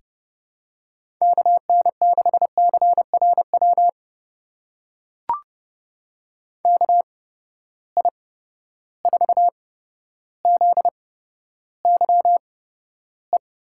Morse code, heard as a keyed tone, about 30 wpm, spaced as for 13 wpm, 700 Hz. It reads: KN6CRW KI4ZYE
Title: USA Call Signs - 6 Characters - 8x Character Spacing - 30wpm